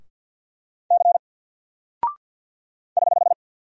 Morse code, heard as a keyed tone, about 48 wpm, 700 Hz. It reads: C <HH>
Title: Single Character (Letter:Number:Punctuation:Prosign) 50wpm